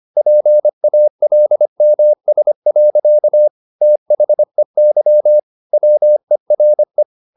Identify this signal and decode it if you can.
Morse code, heard as a keyed tone, about 25 wpm, 600 Hz. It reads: PALMS. THEY WERE